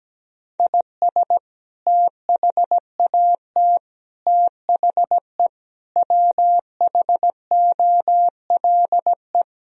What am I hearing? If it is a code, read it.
Morse code, heard as a keyed tone, about 17 wpm, 700 Hz. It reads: IS THAT THE WHOLE
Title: Sentences from Top 1000 Words - Review Part 4 - 17wpm